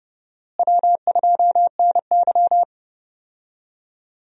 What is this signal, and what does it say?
Morse code, heard as a keyed tone, about 32 wpm, 700 Hz. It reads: W2NY
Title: USA Call Signs - 4 Characters - Repeated 3x - 30wpm